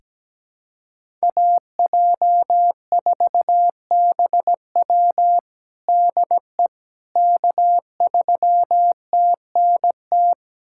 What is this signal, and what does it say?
Morse code, heard as a keyed tone, about 17 wpm, 700 Hz. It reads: AJ4BW DE K3TNT
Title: USA Callsign DE Callsign - 5 Characters - 17wpm